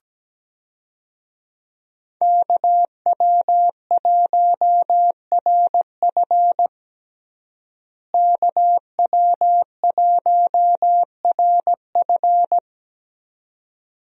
Morse code, heard as a keyed tone, about 17 wpm, 700 Hz. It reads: KW1RF KW1RF
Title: USA Call Signs - 5 Characters - Repeated 2x - 17wpm